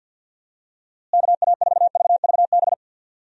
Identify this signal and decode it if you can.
Morse code, heard as a keyed tone, about 46 wpm, 700 Hz. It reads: KA4VVB
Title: USA Call Signs - 6 Characters - Repeated 2x - 50wpm